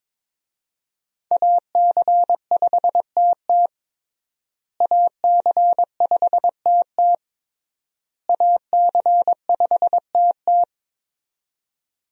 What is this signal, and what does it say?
Morse code, heard as a keyed tone, about 21 wpm, 700 Hz. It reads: AC5TT AC5TT AC5TT
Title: USA Call Signs - 5 Characters - Repeated 3x - 22wpm